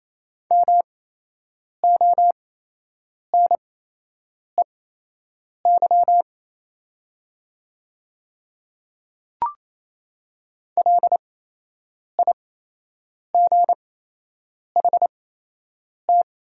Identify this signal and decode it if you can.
Morse code, heard as a keyed tone, about 28 wpm, spaced as for 8 wpm, 700 Hz. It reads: MONEY LIGHT
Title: 8x Character Spacing - Top 500 Words - 28wpm